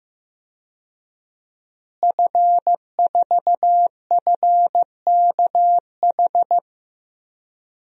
Morse code, heard as a keyed tone, about 15 wpm, 700 Hz. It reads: F4FKH
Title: DXCC-100 Call Signs - 5 Characters - Repeated 3x - 15wpm